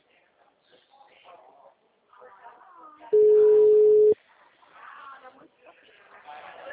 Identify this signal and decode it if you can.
Ringback tone (Europe, 425 Hz), with speech.
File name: out- -2005